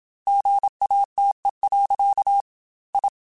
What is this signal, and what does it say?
Morse code, heard as a keyed tone, about 27 wpm, 800 Hz. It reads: GATE. I
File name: Technical Subjects / Technical Subjects-mp3s / AND Gate @25